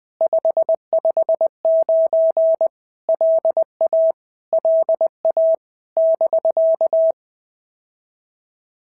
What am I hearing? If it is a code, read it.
Morse code, heard as a keyed tone, about 20 wpm, 650 Hz. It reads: 559 LA LA <BK>
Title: POTA QSO Practice - 20wpm